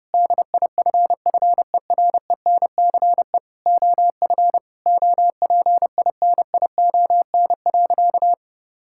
Morse code, heard as a keyed tone, about 32 wpm, 700 Hz. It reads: DIFFERENCE OF OPINION.
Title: Sentences from Top 1600 Words 30wpm